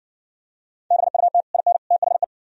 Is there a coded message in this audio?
Yes: Morse code from 0.9 to 2.2 s, heard as a keyed tone, about 60 wpm, 700 Hz. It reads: BUT IN THE